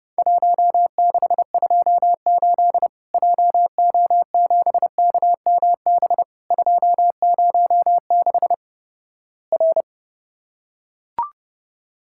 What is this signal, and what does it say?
Morse code, heard as a keyed tone, about 32 wpm, 700 Hz. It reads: 1628 JO7KMB 206 R
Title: Worked All Europe DX Contest - QTC Exchange - 30wpm